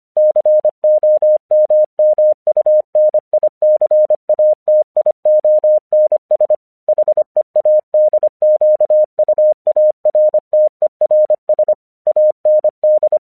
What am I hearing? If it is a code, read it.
Morse code, heard as a keyed tone, about 25 wpm, 610 Hz. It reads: COMMUNICATIONS HEADQUARTERS AND
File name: 1 English 25wpm / AQC07